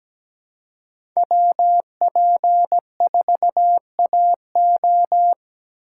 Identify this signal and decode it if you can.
Morse code, heard as a keyed tone, about 17 wpm, 700 Hz. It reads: WP4AO
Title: USA Call Signs - 5 Characters - 4x Character Spacing - 17wpm